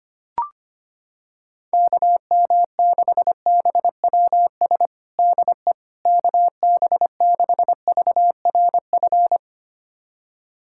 Morse code, heard as a keyed tone, about 25 wpm, 700 Hz. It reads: KM6BWS DE KB6VRF
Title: USA Callsign DE Callsign - 6 Characters - 25wpm